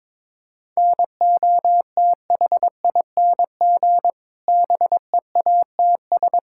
Morse code, heard as a keyed tone, about 22 wpm, 700 Hz. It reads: NOTHING BEATS
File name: Difficult Sentences with Hints - 22wpm